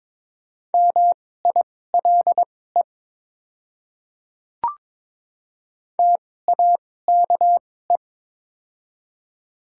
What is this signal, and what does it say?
Morse code, heard as a keyed tone, about 21 wpm, spaced as for 16 wpm, 700 Hz. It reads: MILE TAKE
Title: 2x Character Spacing - Top 300 Words - 22wpm